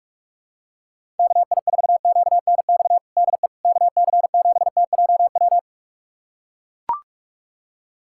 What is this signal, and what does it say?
Morse code, heard as a keyed tone, about 41 wpm, 700 Hz. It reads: KI4QNX DE KC7TJW
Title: USA Callsign DE Callsign - 6 Characters - 45wpm